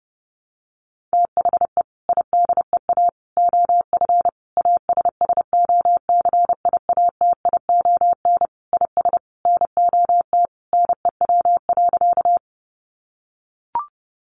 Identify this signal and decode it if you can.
Morse code, heard as a keyed tone, about 30 wpm, 700 Hz. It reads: THE IDEA OF ASSOCIATION IS NOT NEW.
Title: Sentences from Top 1100 Words 30wpm